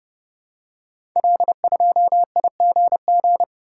Morse code, heard as a keyed tone, about 28 wpm, 700 Hz. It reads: L2IGG
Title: Code Groups - 5 Letter:Numbers - 30wpm